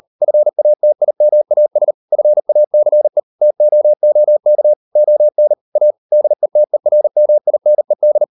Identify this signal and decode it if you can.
Morse code, heard as a keyed tone, about 38 wpm, 600 Hz. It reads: FATIMAS FACE TOOK ON A DETERMINED